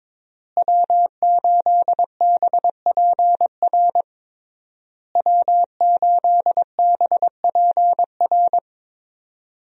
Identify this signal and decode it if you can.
Morse code, heard as a keyed tone, about 22 wpm, 700 Hz. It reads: W8BPR W8BPR
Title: USA Call Signs - 5 Characters - Repeated 3x - 22wpm